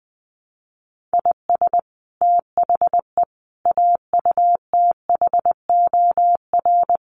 Morse code, heard as a keyed tone, about 20 wpm, 700 Hz. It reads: IS THE AUTHOR
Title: Sets of 4 Words 20wpm